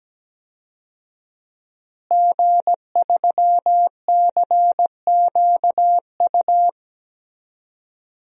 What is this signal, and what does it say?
Morse code, heard as a keyed tone, about 17 wpm, 700 Hz. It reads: G3CQU
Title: DXCC-100 Call Signs - 5 Characters - Repeated 2x - 17wpm